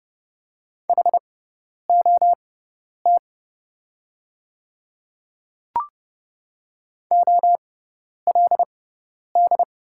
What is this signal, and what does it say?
Morse code, heard as a keyed tone, about 28 wpm, spaced as for 10 wpm, 700 Hz. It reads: HOT OLD
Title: 6x Character Spacing - Top 300 Words - 30wpm